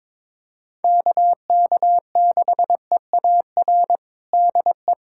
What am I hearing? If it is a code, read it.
Morse code, heard as a keyed tone, about 22 wpm, 700 Hz. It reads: KK6EAR DE
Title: USA Callsign DE Callsign 22wpm